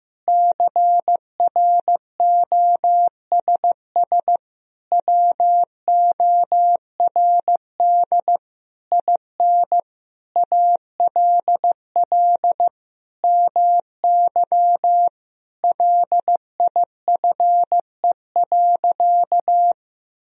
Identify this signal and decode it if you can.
Morse code, heard as a keyed tone, about 15 wpm, 700 Hz. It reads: CROSS WORD IN ALL MY LIFE.
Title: Sentences from Top 1000 Words - Review Part 1 - 15wpm